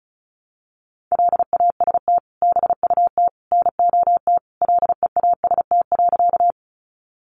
Morse code, heard as a keyed tone, about 34 wpm, 700 Hz. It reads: LAST BUT NOT LEAST.
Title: Sentences from Top 700 Words 35wpm